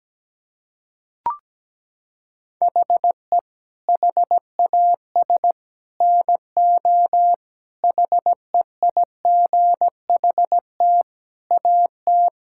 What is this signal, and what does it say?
Morse code, heard as a keyed tone, about 17 wpm, 700 Hz. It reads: HE HAS NO HEIGHT AT
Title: Sentences from Top 1600 Words 17wpm